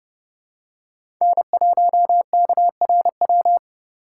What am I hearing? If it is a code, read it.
Morse code, heard as a keyed tone, about 29 wpm, 700 Hz. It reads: N1KRW
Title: USA Call Signs - 5 Characters - 4x Character Spacing - 30wpm